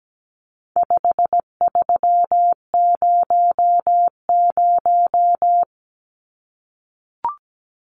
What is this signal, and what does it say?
Morse code, heard as a keyed tone, about 17 wpm, 700 Hz. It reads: 5300
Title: Numbers - 4-Digit 17wpm